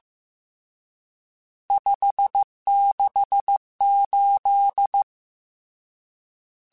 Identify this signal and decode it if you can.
Morse code, heard as a keyed tone, about 15 wpm, 800 Hz. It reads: 568
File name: RSTs / RST-mp3s / Various RST @13